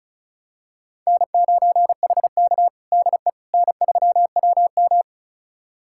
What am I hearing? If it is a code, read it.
Morse code, heard as a keyed tone, about 36 wpm, 700 Hz. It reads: N9HK DE N3WM
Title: USA Callsign DE Callsign - 4 Characters - 35wpm